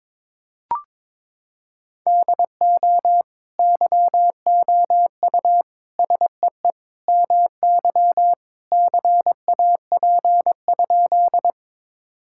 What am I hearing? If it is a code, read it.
Morse code, heard as a keyed tone, about 22 wpm, 700 Hz. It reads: DO YOU SEE MY CAP?
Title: Sentences from Top 1600 Words 22wpm